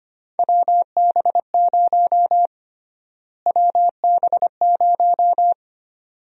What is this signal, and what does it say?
Morse code, heard as a keyed tone, about 25 wpm, 700 Hz. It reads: WB0 WB0
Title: USA Call Sign Prefixes - Repeated 2x - 25wpm